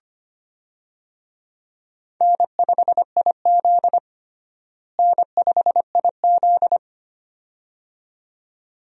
Morse code, heard as a keyed tone, about 25 wpm, 700 Hz. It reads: N5IZ N5IZ
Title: USA Call Signs - 4 Characters - Repeated 2x - 25wpm